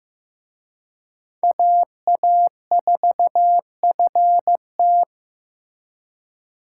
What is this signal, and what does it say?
Morse code, heard as a keyed tone, about 18 wpm, 700 Hz. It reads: AA4FT
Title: Speed Racing - USA 2x2 Call Signs 15wpm